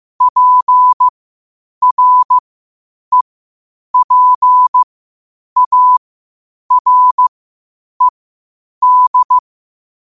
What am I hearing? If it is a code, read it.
Morse code, heard as a keyed tone, about 15 wpm, spaced as for 8 wpm, 1000 Hz. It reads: PREPARED
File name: qotdfiles / 8wpm